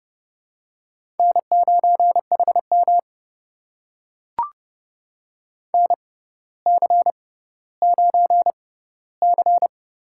Morse code, heard as a keyed tone, about 30 wpm, spaced as for 15 wpm, 700 Hz. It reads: N9HM NC9C
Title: USA Call Signs - 6x Character Spacing - 30wpm